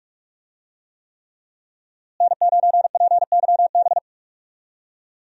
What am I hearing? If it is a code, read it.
Morse code, heard as a keyed tone, about 43 wpm, 700 Hz. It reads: N9PYB
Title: USA Call Signs - 5 Characters - Repeated 3x - 45wpm